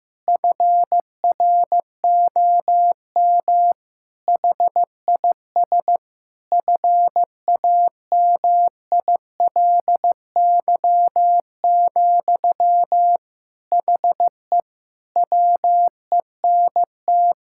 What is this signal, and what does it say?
Morse code, heard as a keyed tone, about 15 wpm, 700 Hz. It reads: FROM HIS FAMILY, HE WENT
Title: Sentences from Top 1000 Words - Review Part 6 - 15wpm